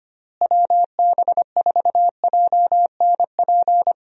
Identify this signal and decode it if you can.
Morse code, heard as a keyed tone, about 25 wpm, 700 Hz. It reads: WB4JNP
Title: USA Call Signs - 6 Characters - Repeated 3x - 25wpm